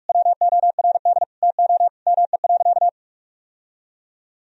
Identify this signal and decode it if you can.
Morse code, heard as a keyed tone, about 46 wpm, 700 Hz. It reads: WORD TO ME.
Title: Sentences from Top 1000 Words - Review Part 4 - 45wpm